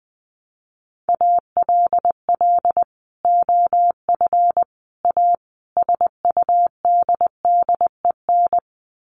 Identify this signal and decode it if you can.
Morse code, heard as a keyed tone, about 20 wpm, 700 Hz. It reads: ALL OF A SUDDEN
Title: Sets of 4 Words 20wpm